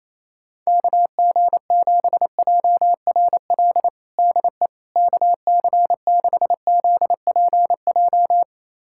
Morse code, heard as a keyed tone, about 28 wpm, 700 Hz. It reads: KG7JRL DE KC6ZPJ
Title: USA Callsign DE Callsign - 6 Characters - 28wpm